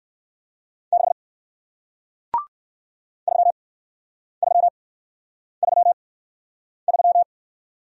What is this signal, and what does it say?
Morse code, heard as a keyed tone, about 57 wpm, spaced as for 9 wpm, 700 Hz. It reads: 6 3333